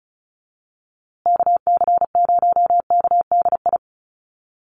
Morse code, heard as a keyed tone, about 34 wpm, 700 Hz. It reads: KC0KDI
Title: USA Call Signs Repeated 3x 35wpm